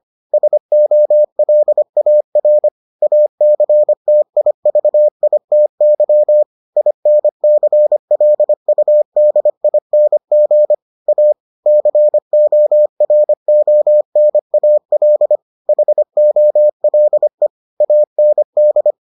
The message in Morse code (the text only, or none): SOLAR ACTIVITY INCLUDING A CORONAL HOLE AND